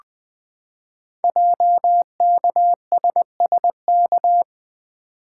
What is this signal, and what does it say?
Morse code, heard as a keyed tone, about 20 wpm, 700 Hz. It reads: JKSSK
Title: Code Groups - 5 Letter:Numbers - 20wpm